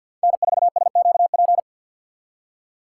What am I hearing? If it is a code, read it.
Morse code, heard as a keyed tone, about 58 wpm, 700 Hz. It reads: N4SQP
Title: Speed Racing - USA 1x3 Call Signs 50wpm